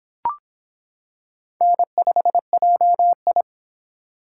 Morse code, heard as a keyed tone, about 25 wpm, 700 Hz. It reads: N5JI